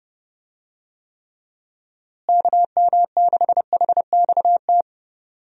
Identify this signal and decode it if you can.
Morse code, heard as a keyed tone, about 30 wpm, 700 Hz. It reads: KM6HXT
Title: USA Call Signs - 6 Characters - Repeated 2x - 30wpm